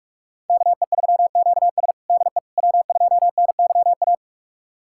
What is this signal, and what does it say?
Morse code, heard as a keyed tone, about 44 wpm, 700 Hz. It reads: KE3QS DE W2NYA
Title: USA Callsign DE Callsign - 5 Characters - 45wpm